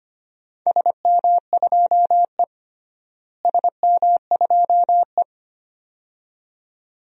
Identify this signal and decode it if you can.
Morse code, heard as a keyed tone, about 25 wpm, 700 Hz. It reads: SM2E SM2E
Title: DXCC-100 Call Signs - 4 Characters - Repeated 2x - 25wpm